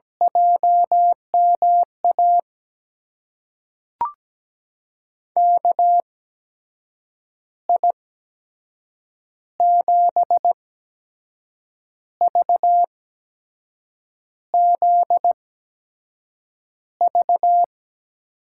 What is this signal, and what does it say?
Morse code, heard as a keyed tone, about 17 wpm, spaced as for 6 wpm, 700 Hz. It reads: JMA KI7VZV
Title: USA Call Signs - 6 Characters - 8x Character Spacing - 17wpm